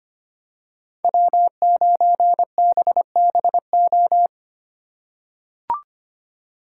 Morse code, heard as a keyed tone, about 25 wpm, 700 Hz. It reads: W9BBO
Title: USA Call Signs - 5 Characters - 4x Character Spacing - 25wpm